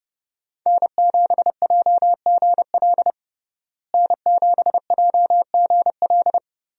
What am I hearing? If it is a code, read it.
Morse code, heard as a keyed tone, about 28 wpm, 700 Hz. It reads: N7JGL N7JGL
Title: USA Call Signs - 5 Characters - Repeated 3x - 30wpm